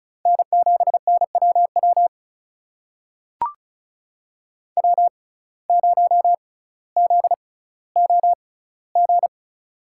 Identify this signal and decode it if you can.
Morse code, heard as a keyed tone, about 34 wpm, spaced as for 18 wpm, 700 Hz. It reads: N7NWW W0ZOG
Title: USA Call Signs - 5 Characters - 6x Character Spacing - 35wpm